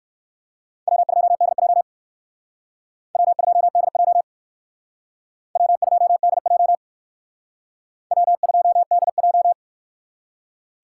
Morse code, heard as a keyed tone, about 58 wpm, 700 Hz. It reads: W2DJ W2DJ W2DJ W2DJ